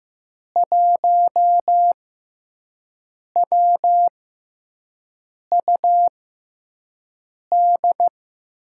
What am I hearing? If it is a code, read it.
Morse code, heard as a keyed tone, about 16 wpm, spaced as for 5 wpm, 700 Hz. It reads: 1WUD
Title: USA Call Signs - 5 Characters - 6x Character Spacing - 15wpm